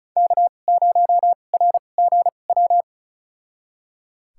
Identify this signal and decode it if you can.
Morse code, heard as a keyed tone, about 36 wpm, spaced as for 25 wpm, 690 Hz. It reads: K0RGW
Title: USA Call Signs - 5 Characters - 2x Character Spacing - 35wpm